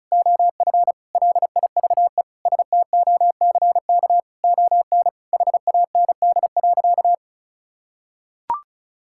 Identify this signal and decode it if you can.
Morse code, heard as a keyed tone, about 36 wpm, 700 Hz. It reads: OF LIVE STOCK ON HAND.
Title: Sentences from Top 1000 Words - Review Part 5 - 35wpm